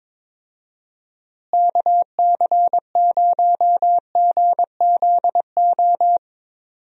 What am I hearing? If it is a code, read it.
Morse code, heard as a keyed tone, about 22 wpm, 700 Hz. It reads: KC0GZO
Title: USA Call Signs - 6 Characters - Repeated 2x - 22wpm